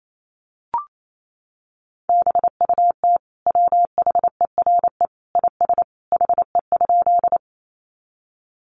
Morse code, heard as a keyed tone, about 28 wpm, 700 Hz. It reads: BUT WHERE IS HE?
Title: Sentences from Top 200 Words 28wpm